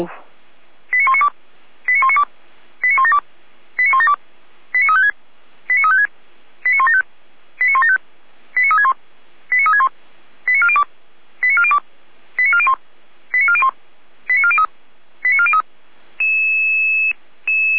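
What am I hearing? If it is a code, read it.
Five-tone selective call (ZVEI), heard as quick runs of five tones, tones of about 80 ms. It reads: 89192 89192 89172 89172 89356 89356 89175 89175 89361 89361 89402 89402 89401 89401 89403 89403
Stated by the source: ZVEI five-tone selective call received by SDR.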